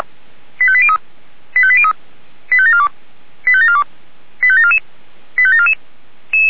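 Five-tone selective call (ZVEI), heard as quick runs of five tones, tones of about 70 ms. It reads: 85803 85803 85742 85742 85744 85744
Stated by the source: ZVEI five-tone selective call received via SDR.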